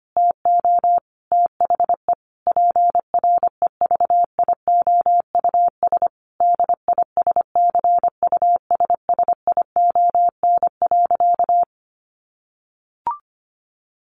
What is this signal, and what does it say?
Morse code, heard as a keyed tone, about 25 wpm, 700 Hz. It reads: TO THE PREVIOUS DISCUSSION.
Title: Sentences from Top 1500 Words 25wpm